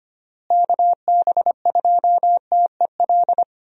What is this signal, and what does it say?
Morse code, heard as a keyed tone, about 25 wpm, 700 Hz. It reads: KB2TEL
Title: USA Call Signs - 6 Characters - Repeated 3x - 25wpm